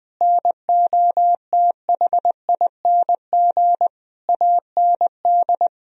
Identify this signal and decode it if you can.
Morse code, heard as a keyed tone, about 20 wpm, 700 Hz. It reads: NOTHING AND
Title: Sentences from Top 1000 Words - Review Part 3 - 20wpm